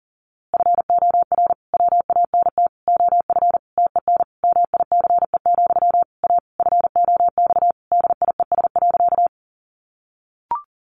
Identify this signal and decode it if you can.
Morse code, heard as a keyed tone, about 40 wpm, 700 Hz. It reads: FOR WANT OF TEN MICE, A FOX DIES.
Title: A Sound Of Thunder 40wpm